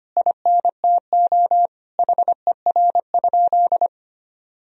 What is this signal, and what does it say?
Morse code, heard as a keyed tone, about 25 wpm, 700 Hz. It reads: INTO HER?
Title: Sentences from Top 1000 Words - Review Part 1 - 25wpm